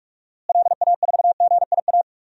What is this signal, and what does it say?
Morse code, heard as a keyed tone, about 47 wpm, 700 Hz. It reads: LA4GIA
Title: DXCC-100 Call Signs - 6 Characters - Repeated 2x - 45wpm